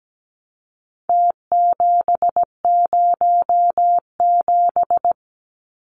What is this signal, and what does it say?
Morse code, heard as a keyed tone, about 17 wpm, 700 Hz. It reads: T707
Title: Numbers - Contest Style - 4-Digit 17wpm